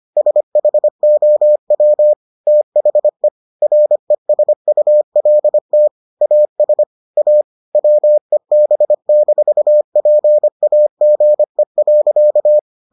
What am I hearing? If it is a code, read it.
Morse code, heard as a keyed tone, about 25 wpm, 600 Hz. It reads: SHOW THE RESULT AS A WEB-PAGE.